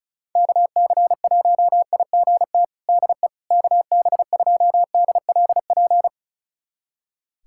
Morse code, heard as a keyed tone, about 36 wpm, 700 Hz. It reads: KC1IGT DE KB2DLP
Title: USA Callsign DE Callsign - 6 Characters - 35wpm